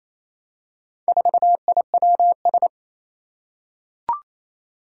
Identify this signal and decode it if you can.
Morse code, heard as a keyed tone, about 27 wpm, 700 Hz. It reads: 4IWS
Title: Four Challenging Letter-Number Combinations 28wpm